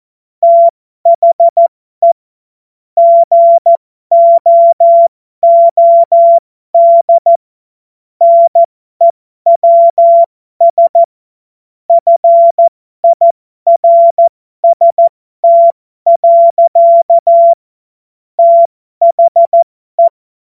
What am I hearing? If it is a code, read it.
Morse code, heard as a keyed tone, about 14 wpm, spaced as for 12 wpm, 680 Hz. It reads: THE GOOD NEWS FIRST. THE
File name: Morsekurs / Witze